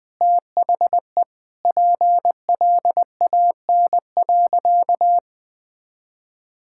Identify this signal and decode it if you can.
Morse code, heard as a keyed tone, about 20 wpm, 700 Hz. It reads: THE PLAN.
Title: Sentences with 4-Word Phrases - 20wpm